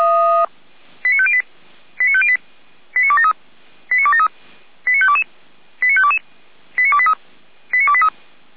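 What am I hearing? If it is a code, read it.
Five-tone selective call (ZVEI), heard as quick runs of five tones, tones of about 70 ms. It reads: 89508 89508 89273 89273 89522 89522 89283 89283
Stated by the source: Five-tone ZVEI selective call received on SDR, recorded by fire brigade Feuerwehr Orlamünde.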